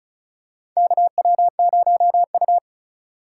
Morse code, heard as a keyed tone, about 36 wpm, 700 Hz. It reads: KW0U
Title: USA Call Signs - 4 Characters - 2x Character Spacing - 35wpm